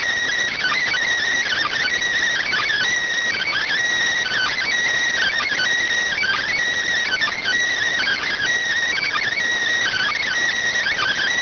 Emergency Alert System (SAME) tones (header bursts), heard as harsh buzzing bursts. It is heard from 0.5 to 10.3 s.